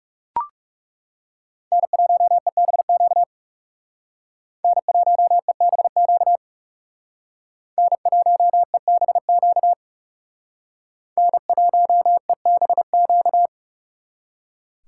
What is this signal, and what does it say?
Morse code, heard as a keyed tone, about 38 wpm, 700 Hz. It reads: N1EBQ N1EBQ N1EBQ N1EBQ